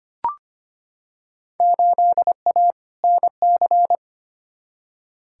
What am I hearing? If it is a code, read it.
Morse code, heard as a keyed tone, about 25 wpm, 700 Hz. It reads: 8A NC
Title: Field Day Exchanges - 25wpm